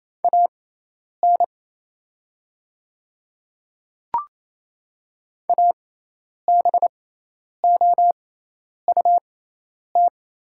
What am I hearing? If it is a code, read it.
Morse code, heard as a keyed tone, about 28 wpm, spaced as for 10 wpm, 690 Hz. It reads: AN ABOUT
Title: 6x Character Spacing - Top 300 Words - 28wpm